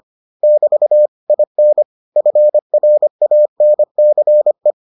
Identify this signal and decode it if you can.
Morse code, heard as a keyed tone, about 25 wpm, 600 Hz. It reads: <BT> IN FRANCE